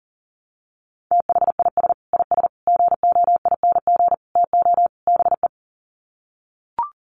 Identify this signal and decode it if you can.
Morse code, heard as a keyed tone, about 40 wpm, 700 Hz. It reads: THIS IS GOING TO BE
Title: Sets of 5 Words 40wpm